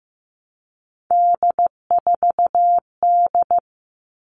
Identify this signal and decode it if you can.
Morse code, heard as a keyed tone, about 15 wpm, 700 Hz. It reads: D4D
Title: All Cut Numbers - 3-Digit 15wpm